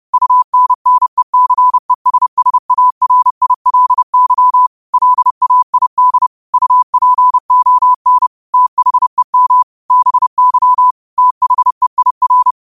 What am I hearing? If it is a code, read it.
Morse code, heard as a keyed tone, about 30 wpm, 1000 Hz. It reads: UNNECESSARILY LAID UPON THEM BY THEIR